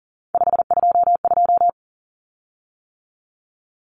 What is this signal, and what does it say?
Morse code, heard as a keyed tone, about 40 wpm, 700 Hz. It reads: 522
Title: Common Cut Numbers - 3-Digit 40wpm